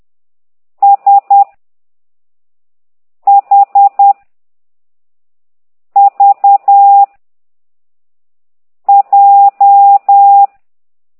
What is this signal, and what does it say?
Morse code, heard as a keyed tone, about 10 wpm, 800 Hz. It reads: S H V J
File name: ct2idw / exerc15